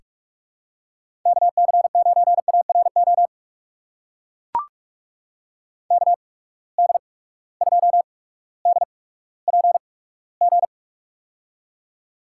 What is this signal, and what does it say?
Morse code, heard as a keyed tone, about 41 wpm, spaced as for 19 wpm, 700 Hz. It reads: KC9ARO KD2DPG
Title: USA Call Signs - 6 Characters - 8x Character Spacing - 45wpm